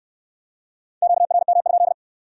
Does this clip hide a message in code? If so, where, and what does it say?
Yes: Morse code from 1.0 to 1.9 s, heard as a keyed tone, about 64 wpm, 700 Hz. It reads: QRM?